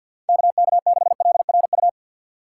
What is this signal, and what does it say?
Morse code, heard as a keyed tone, about 52 wpm, 700 Hz. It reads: KK6LRU
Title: USA Call Signs - 6 Characters - 4x Character Spacing - 50wpm